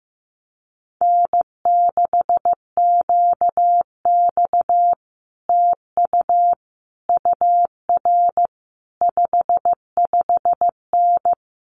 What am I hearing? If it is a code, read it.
Morse code, heard as a keyed tone, about 15 wpm, 700 Hz. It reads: N6QX TU UR 55N
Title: Common QSO Phrases Encore 15wpm